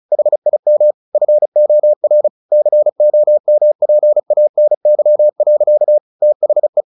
Morse code, heard as a keyed tone, about 34 wpm, 600 Hz. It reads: HIM FOR COMPANY. THE
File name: the old man and the sea / the old man and the sea 1 35wpm